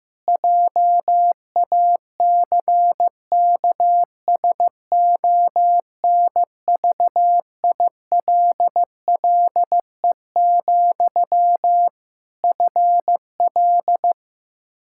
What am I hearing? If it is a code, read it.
Morse code, heard as a keyed tone, about 15 wpm, 700 Hz. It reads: JACKSONVILLE, FL
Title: Practice QSO - Exchange 1 - Beginning - 15wpm